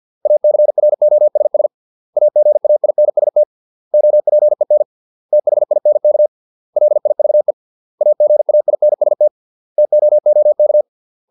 Morse code, heard as a keyed tone, about 50 wpm, 600 Hz. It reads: ACROSS AGAINST OPEN THINK LIVE AGAINST TOOK